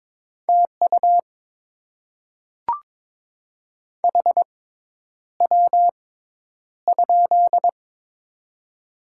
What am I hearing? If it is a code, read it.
Morse code, heard as a keyed tone, about 22 wpm, spaced as for 10 wpm, 700 Hz. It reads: TU HW?
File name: Top 20 QSO Elements - 6x Character Spacing - 22wpm